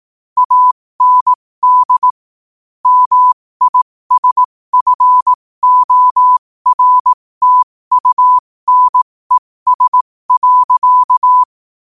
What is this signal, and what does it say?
Morse code, heard as a keyed tone, about 18 wpm, spaced as for 15 wpm, 1000 Hz. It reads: AND MISFORTUNES.
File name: qotdfiles / 15wpm